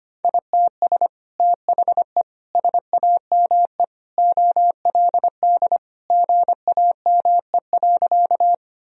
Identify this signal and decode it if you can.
Morse code, heard as a keyed tone, about 25 wpm, 700 Hz. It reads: ITS THE SAME OLD GAME.
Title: Sentences from Top 1000 Words - Review Part 3 - 25wpm